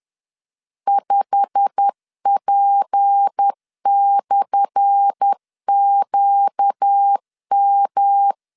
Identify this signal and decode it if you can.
Morse code, heard as a keyed tone, about 10 wpm, 800 Hz. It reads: 5P/QM
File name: LEZIONI / 10 SIGNS / 10C DOT 10wpm 2940